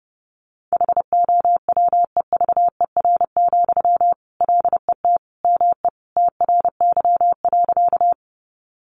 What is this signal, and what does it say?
Morse code, heard as a keyed tone, about 30 wpm, 700 Hz. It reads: HOWEVER, LET ME TRY.
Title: Sentences from Top 600 Words 30wpm